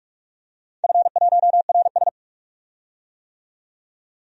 Morse code, heard as a keyed tone, about 44 wpm, 700 Hz. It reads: F1RS
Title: DXCC-100 Call Signs - 4 Characters - 45wpm